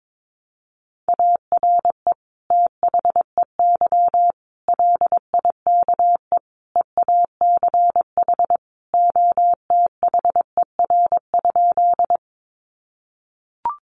Morse code, heard as a keyed tone, about 22 wpm, 700 Hz. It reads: ARE THEY LIKE EACH OTHER?
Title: Sentences from Top 200 Words 22wpm